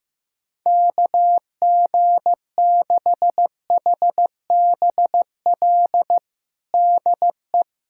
Morse code, heard as a keyed tone, about 15 wpm, 700 Hz. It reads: KG6HBL DE
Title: Practice QSO - Exchange 4 - Ending - 15wpm